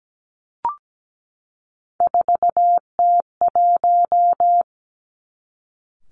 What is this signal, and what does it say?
Morse code, heard as a keyed tone, about 17 wpm, 700 Hz. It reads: 4T1
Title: Three Challenging Letter-Number Combinations 17wpm